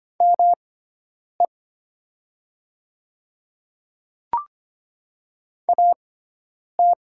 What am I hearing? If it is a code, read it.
Morse code, heard as a keyed tone, about 25 wpm, spaced as for 9 wpm, 690 Hz. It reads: ME AT
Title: 6x Character Spacing - Top 200 Words - 25wpm